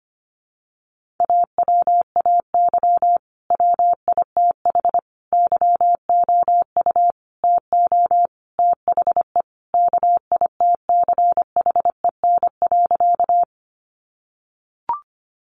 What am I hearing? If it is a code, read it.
Morse code, heard as a keyed tone, about 25 wpm, 700 Hz. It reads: AWAY WITH YOU TO THE KITCHEN.
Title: Sentences from Top 1400 Words 25wpm